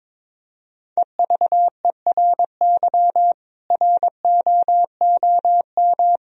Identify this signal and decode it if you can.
Morse code, heard as a keyed tone, about 22 wpm, 700 Hz. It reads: EVERY ROOM
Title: Sentences from Top 1000 Words - Review Part 3 - 22wpm